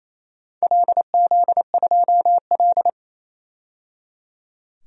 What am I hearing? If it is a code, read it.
Morse code, heard as a keyed tone, about 28 wpm, 700 Hz. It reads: LZ2L
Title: DXCC-100 Call Signs - 4 Characters - 28wpm